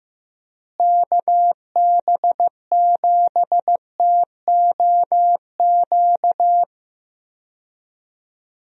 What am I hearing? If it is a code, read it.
Morse code, heard as a keyed tone, about 15 wpm, 700 Hz. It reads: KB7TOQ
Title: USA Call Signs - 6 Characters - Repeated 2x - 15wpm